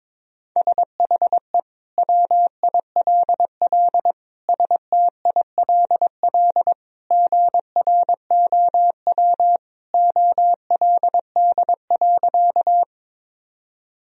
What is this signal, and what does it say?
Morse code, heard as a keyed tone, about 22 wpm, 700 Hz. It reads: SHE WILL STILL GROW OLD.
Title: Sentences from Top 1000 Words - Review Part 1 - 22wpm